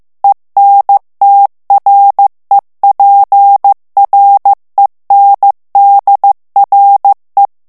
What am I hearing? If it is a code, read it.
Morse code, heard as a keyed tone, about 15 wpm, 800 Hz. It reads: ENTREPRENDRE
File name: morse mp3 / FRANCAIS TEXTE CLAIR 1 VITESSE 14 MOTS MINUTES